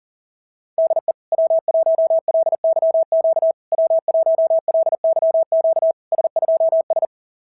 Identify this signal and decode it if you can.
Morse code, heard as a keyed tone, about 40 wpm, 650 Hz. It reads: DE W1LYQ W1LYQ S2S
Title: SOTA QSO Practice 40wpm